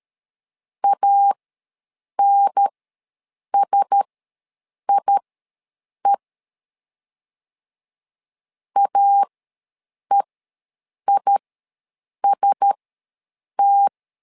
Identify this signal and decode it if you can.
Morse code, heard as a keyed tone, about 13 wpm, spaced as for 7 wpm, 800 Hz. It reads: ANSIE AEIST
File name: LEZIONI / 4 Q1S ws 12wpm 1827